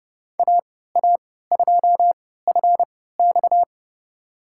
Morse code, heard as a keyed tone, about 30 wpm, spaced as for 17 wpm, 700 Hz. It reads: AA2FX
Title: USA Call Signs - 5 Characters - 3x Character Spacing - 30wpm